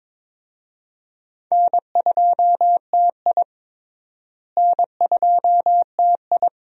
Morse code, heard as a keyed tone, about 22 wpm, 700 Hz. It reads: N2TI N2TI
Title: USA Call Signs - 4 Characters - Repeated 2x - 22wpm